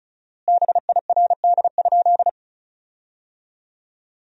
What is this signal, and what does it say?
Morse code, heard as a keyed tone, about 36 wpm, 700 Hz. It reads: BIRD?
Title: 4x Word-Spacing - Sentences from Top 400 Words - 35wpm